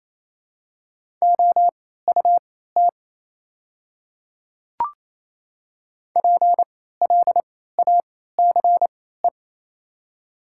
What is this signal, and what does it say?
Morse code, heard as a keyed tone, about 27 wpm, spaced as for 16 wpm, 690 Hz. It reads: OUT PLACE
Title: 3x Character Spacing - Top 200 Words - 28wpm